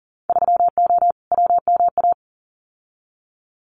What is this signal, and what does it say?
Morse code, heard as a keyed tone, about 40 wpm, 700 Hz. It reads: 3O WMA
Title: Winter Field Day Exchanges Encore 40wpm